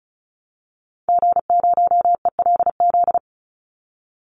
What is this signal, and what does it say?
Morse code, heard as a keyed tone, about 34 wpm, 700 Hz. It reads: G0ELZ
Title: Call Signs Repeated 3x - 35wpm